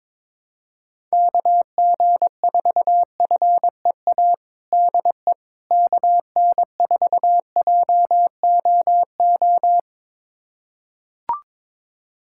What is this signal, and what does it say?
Morse code, heard as a keyed tone, about 22 wpm, 700 Hz. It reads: KG4FEA DE KN4JOO
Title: USA Callsign DE Callsign - 6 Characters - 22wpm